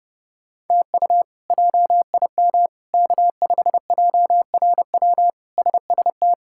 Morse code, heard as a keyed tone, about 32 wpm, 700 Hz. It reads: TU JIM K5JRW SST
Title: K1USN SST QSO Practice - 30wpm